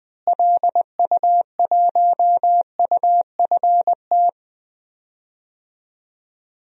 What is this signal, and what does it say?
Morse code, heard as a keyed tone, about 20 wpm, 700 Hz. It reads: LU1UFT
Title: DXCC-100 Call Signs - 6 Characters - Repeated 2x - 20wpm